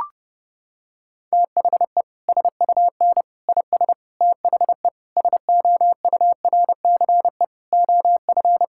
Morse code, heard as a keyed tone, about 30 wpm, 700 Hz. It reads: THE SUN IS THE SOURCE OF
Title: Sentences from Top 1000 Words - Review Part 5 - 30wpm